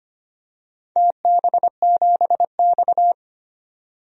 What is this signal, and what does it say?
Morse code, heard as a keyed tone, about 25 wpm, 700 Hz. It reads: TB7X
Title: DXCC-100 Call Signs - 4 Characters - 25wpm